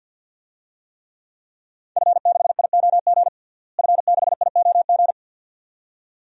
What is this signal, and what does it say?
Morse code, heard as a keyed tone, about 52 wpm, 700 Hz. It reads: F6IOG F6IOG
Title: DXCC-100 Call Signs - 5 Characters - Repeated 2x - 50wpm